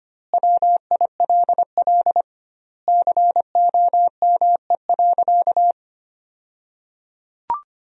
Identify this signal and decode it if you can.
Morse code, heard as a keyed tone, about 25 wpm, 700 Hz. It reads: WILL COME.
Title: 2x Word-Spacing - Sentences from Top 100 Words - 25wpm